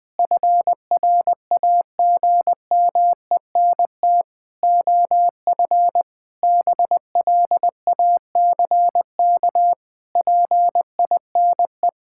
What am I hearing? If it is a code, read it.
Morse code, heard as a keyed tone, about 20 wpm, 700 Hz. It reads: FRAGMENT OF BLACK PINE
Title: Pangram Challenge v2 - 20wpm